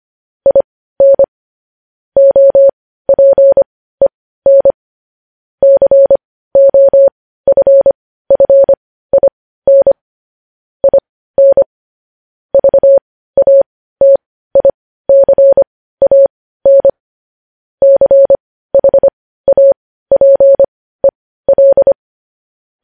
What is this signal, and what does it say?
Morse code, heard as a keyed tone, about 25 wpm, spaced as for 15 wpm, 550 Hz. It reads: IN OPEN COFFIN IN VATICAN CHAPEL
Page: News headlines in morse code at 15 wpm